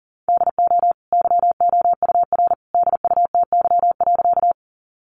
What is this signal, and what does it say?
Morse code, heard as a keyed tone, about 40 wpm, 700 Hz. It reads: DO YOUR DUTY.
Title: Sentences from Top 1500 Words 40wpm